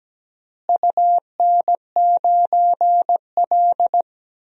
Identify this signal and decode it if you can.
Morse code, heard as a keyed tone, about 17 wpm, 700 Hz. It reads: UN9L
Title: DXCC-100 Call Signs - 4 Characters - Repeated 3x - 17wpm